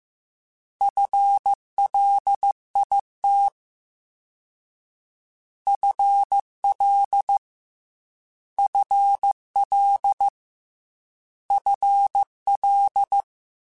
Morse code, heard as a keyed tone, about 15 wpm, 800 Hz. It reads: FLIT FL FL FL
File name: FL / FL mp3s / FL Prefixes 3x2 @13